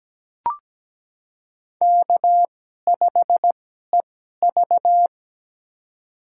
Morse code, heard as a keyed tone, about 17 wpm, spaced as for 12 wpm, 700 Hz. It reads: K5EV
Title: USA Call Signs - 4 Characters - 2x Character Spacing - 17wpm